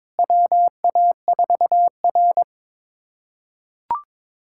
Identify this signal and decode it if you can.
Morse code, heard as a keyed tone, about 22 wpm, 700 Hz. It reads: WA4R
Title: USA Callsign DE Callsign - 4 Characters - 22wpm